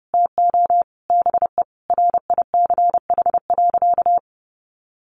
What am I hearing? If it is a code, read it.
Morse code, heard as a keyed tone, about 28 wpm, 700 Hz. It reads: TO BE RICH.
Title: Sentences from Top 1100 Words 30wpm